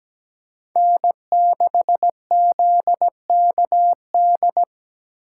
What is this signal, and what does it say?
Morse code, heard as a keyed tone, about 20 wpm, 700 Hz. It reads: N6ZKD
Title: Speed Racing - USA 1x3 Call Signs 17wpm